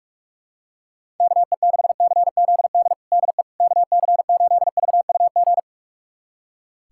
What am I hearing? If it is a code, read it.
Morse code, heard as a keyed tone, about 46 wpm, 690 Hz. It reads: KE6CZD DE KC8VUG
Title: USA Callsign DE Callsign - 6 Characters - 45wpm